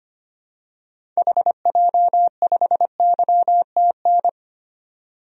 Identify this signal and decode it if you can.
Morse code, heard as a keyed tone, about 25 wpm, 700 Hz. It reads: HJ5YTN
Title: DXCC-100 Call Signs - 6 Characters - 25wpm